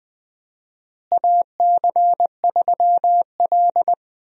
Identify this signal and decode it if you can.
Morse code, heard as a keyed tone, about 20 wpm, 700 Hz. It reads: AC3L
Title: USA Call Signs - 4 Characters - Repeated 3x - 20wpm